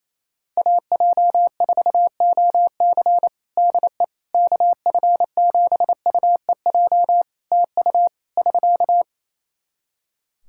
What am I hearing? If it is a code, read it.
Morse code, heard as a keyed tone, about 28 wpm, 700 Hz. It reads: AJ4OC DE KF7UEJ TU <SK>
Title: Practice QSO - Exchange 4 - Ending - 28wpm